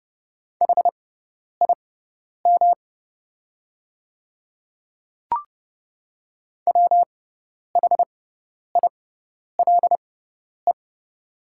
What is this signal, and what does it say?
Morse code, heard as a keyed tone, about 29 wpm, spaced as for 10 wpm, 700 Hz. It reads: HIM WHILE
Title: 6x Character Spacing - Top 300 Words - 30wpm